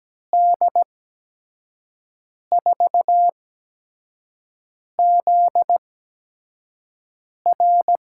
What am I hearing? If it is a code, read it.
Morse code, heard as a keyed tone, about 17 wpm, spaced as for 5 wpm, 700 Hz. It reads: D4ZR
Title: USA Call Signs - 8x Character Spacing - 17wpm